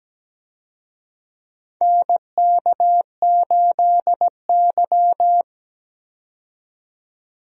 Morse code, heard as a keyed tone, about 17 wpm, 700 Hz. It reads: NK8Y
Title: USA Call Signs - 4 Characters - 17wpm